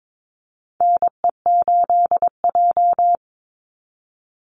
Morse code, heard as a keyed tone, about 22 wpm, 700 Hz. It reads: NE8J
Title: Call Signs - 22wpm